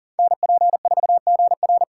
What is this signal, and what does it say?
Morse code, heard as a keyed tone, about 40 wpm, 690 Hz. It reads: NP4GR
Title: USA Call Signs - 3x Character Spacing - 40wpm